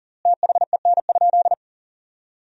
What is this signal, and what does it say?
Morse code, heard as a keyed tone, about 40 wpm, 700 Hz. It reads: THEN?
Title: 4x Word-Spacing - Sentences from Top 100 Words - 40wpm